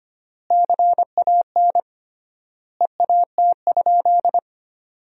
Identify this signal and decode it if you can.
Morse code, heard as a keyed tone, about 25 wpm, 700 Hz. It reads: CAN EAT?
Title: 3x Word-Spacing - Sentences from Top 500 Words - 25wpm